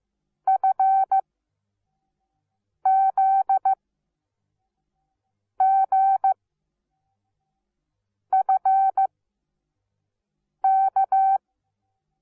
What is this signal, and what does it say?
Morse code, heard as a keyed tone, about 15 wpm, spaced as for 4 wpm, 770 Hz. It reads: FZGFK
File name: Onlinelehrgang / morsen / DJ4UF Morsen17